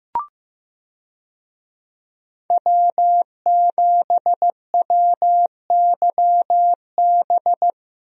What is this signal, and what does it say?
Morse code, heard as a keyed tone, about 15 wpm, 700 Hz. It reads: W7WYB
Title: USA Call Signs - 5 Characters - 15wpm